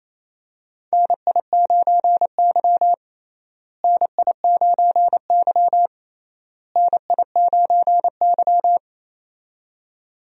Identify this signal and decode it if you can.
Morse code, heard as a keyed tone, about 28 wpm, 700 Hz. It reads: NI9Y NI9Y NI9Y
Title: USA Call Signs - 4 Characters - Repeated 3x - 28wpm